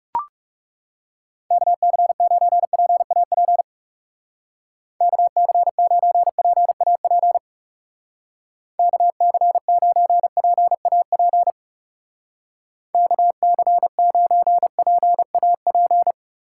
Morse code, heard as a keyed tone, about 35 wpm, 700 Hz. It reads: KC9PAP KC9PAP KC9PAP KC9PAP
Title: Speed Racing - USA 2x3 Call Signs 30wpm